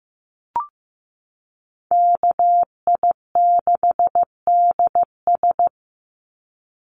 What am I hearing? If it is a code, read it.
Morse code, heard as a keyed tone, about 15 wpm, 700 Hz. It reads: KI6DS
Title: NA SOTA Chaser Callsigns 15wpm